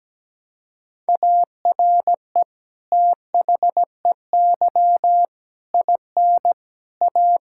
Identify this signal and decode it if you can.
Morse code, heard as a keyed tone, about 17 wpm, 700 Hz. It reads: ARE THEY IN A
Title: Sentences from Top 1000 Words - Review Part 6 - 17wpm